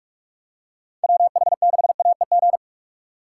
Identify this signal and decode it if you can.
Morse code, heard as a keyed tone, about 44 wpm, 700 Hz. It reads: WH6AEG